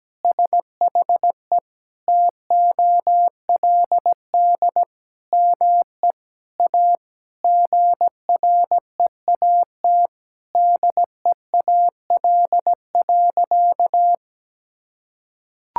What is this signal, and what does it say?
Morse code, heard as a keyed tone, about 17 wpm, 700 Hz. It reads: SHE TOLD ME A GREAT DEAL.
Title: Sentences from Top 1000 Words - Review Part 6 - 17wpm